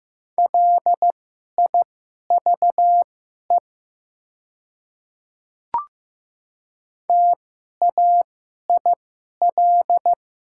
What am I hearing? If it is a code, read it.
Morse code, heard as a keyed tone, about 15 wpm, spaced as for 11 wpm, 700 Hz. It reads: LIVE TAIL
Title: 2x Character Spacing - Top 500 Words - 15wpm